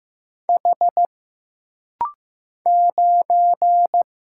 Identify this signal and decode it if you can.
Morse code, heard as a keyed tone, about 15 wpm, 700 Hz. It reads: H 9
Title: Single Letter-Number Alt - Rapid-Fire 15wpm